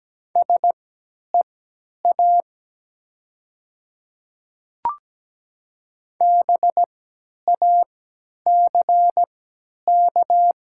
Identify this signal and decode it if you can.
Morse code, heard as a keyed tone, about 17 wpm, spaced as for 10 wpm, 690 Hz. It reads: SEA BACK
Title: 3x Character Spacing - Top 300 Words - 17wpm